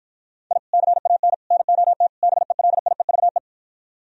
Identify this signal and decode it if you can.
Morse code, heard as a keyed tone, about 52 wpm, 700 Hz. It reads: I CAN NOT BELIEVE